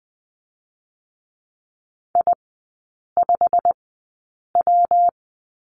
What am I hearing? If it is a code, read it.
Morse code, heard as a keyed tone, about 20 wpm, spaced as for 8 wpm, 700 Hz. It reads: I5W
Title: 127 Lesson 21e - Number 6 - Three Characters 20wpm